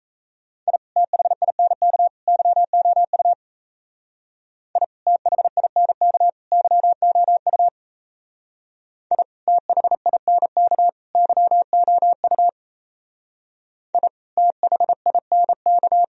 Morse code, heard as a keyed tone, about 32 wpm, 700 Hz. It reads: I THINK YOU I THINK YOU I THINK YOU I THINK
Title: Speed-Racing - Sets of 3 Words 28wpm